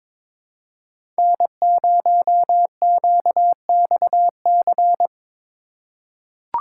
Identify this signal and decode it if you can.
Morse code, heard as a keyed tone, about 22 wpm, 700 Hz. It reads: N0QXC
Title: USA Call Signs - 5 Characters - 3x Character Spacing - 22wpm